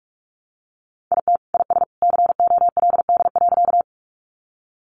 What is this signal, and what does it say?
Morse code, heard as a keyed tone, about 46 wpm, 700 Hz. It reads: IT IS COLD.
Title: Sentences from Top 400 Words 45wpm